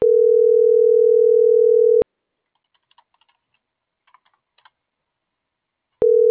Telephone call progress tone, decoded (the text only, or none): ringback tone